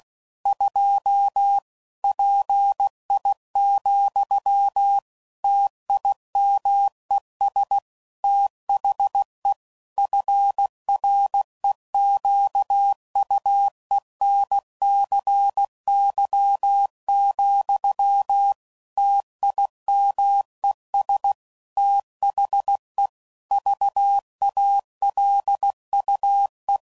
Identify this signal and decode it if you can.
Morse code, heard as a keyed tone, about 16 wpm, 800 Hz. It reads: 2 PI, TIMES THE FREQUENCY, TIMES THE VALUE
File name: CW Code Groups @15WPM/Technical Subjects